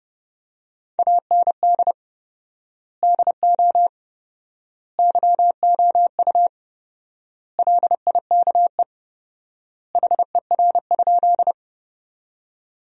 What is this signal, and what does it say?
Morse code, heard as a keyed tone, about 32 wpm, 700 Hz. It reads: AND DO YOU LIKE HER?
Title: 4x Word-Spacing - Sentences from Top 100 Words - 30wpm